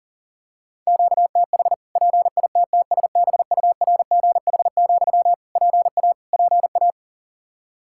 Morse code, heard as a keyed tone, about 40 wpm, 700 Hz. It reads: QTH PITTSBURGH, PA PA
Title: Practice QSO - Exchange 1 - Beginning - 40wpm